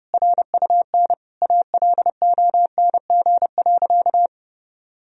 Morse code, heard as a keyed tone, about 32 wpm, 700 Hz. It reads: RUN ALONG.
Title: Sentences from Top 1000 Words - Review Part 1 - 30wpm